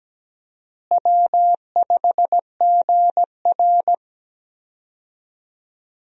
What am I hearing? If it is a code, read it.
Morse code, heard as a keyed tone, about 17 wpm, 700 Hz. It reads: W5GR
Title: USA Call Signs - 4 Characters - Repeated 3x - 17wpm